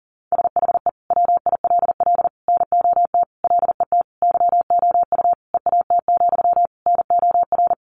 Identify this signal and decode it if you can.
Morse code, heard as a keyed tone, about 40 wpm, 700 Hz. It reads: SHE WILL NOT LET YOU EAT, NOR
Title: Sentences from Top 1500 Words 40wpm